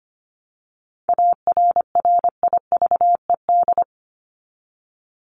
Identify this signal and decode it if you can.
Morse code, heard as a keyed tone, about 25 wpm, 700 Hz. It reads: ARRIVED
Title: ED Suffix - Top 100 Words with Base Words 25wpm